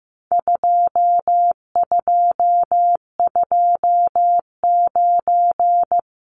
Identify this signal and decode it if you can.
Morse code, heard as a keyed tone, about 15 wpm, 700 Hz. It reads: 2229
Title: Numbers - 4-Digit 15wpm